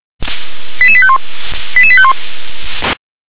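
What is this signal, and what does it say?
Five-tone selective call (ZVEI), heard as quick runs of five tones, tones of about 70 ms. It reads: 88851 88851
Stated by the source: ZVEI five-tone selective call received on an SDR.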